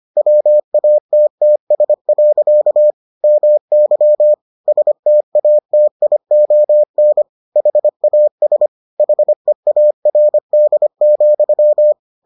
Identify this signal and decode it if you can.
Morse code, heard as a keyed tone, about 25 wpm, 600 Hz. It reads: WATTS. MY STATION HAS HEARD,